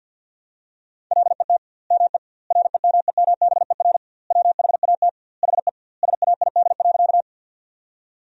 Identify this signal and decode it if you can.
Morse code, heard as a keyed tone, about 50 wpm, 700 Hz. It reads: LET ME REMEMBER WHAT HE SAID.
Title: 2x Word-Spacing - Sentences from Top 400 Words - 50wpm